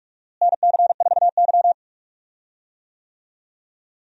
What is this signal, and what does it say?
Morse code, heard as a keyed tone, about 46 wpm, 700 Hz. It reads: NC4Y
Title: USA Call Signs - 4 Characters - Repeated 2x - 45wpm